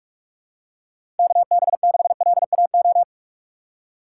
Morse code, heard as a keyed tone, about 42 wpm, 700 Hz. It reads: KB6LAO
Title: USA Call Signs - 6 Characters - 4x Character Spacing - 45wpm